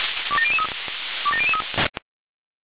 Five-tone selective call (ZVEI), heard as quick runs of five tones, tones of about 70 ms. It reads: 27003 27003